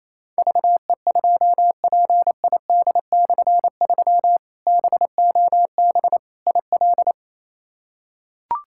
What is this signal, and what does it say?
Morse code, heard as a keyed tone, about 28 wpm, 700 Hz. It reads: VE2PID/3 BOB IL
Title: K1USN SST QSO Practice - 28wpm